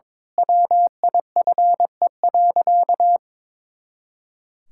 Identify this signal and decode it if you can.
Morse code, heard as a keyed tone, about 22 wpm, 700 Hz. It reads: WIFE.
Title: Pangram Challenge v2 - 22wpm